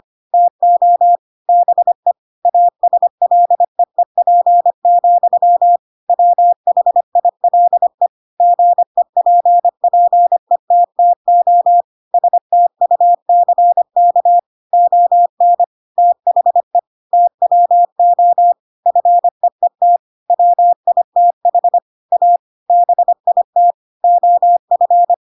Morse code, heard as a keyed tone, about 25 wpm, 700 Hz. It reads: TO BE ASLEEP, WHILE GEPPETTO STUCK ON THE TWO FEET WITH A BIT OF